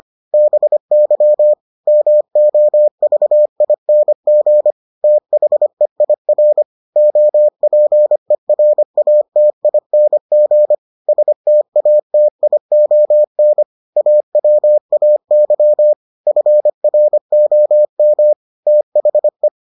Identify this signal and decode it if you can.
Morse code, heard as a keyed tone, about 25 wpm, 600 Hz. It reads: BY MOVING THEIR OPERATING STATION AWAY FROM THE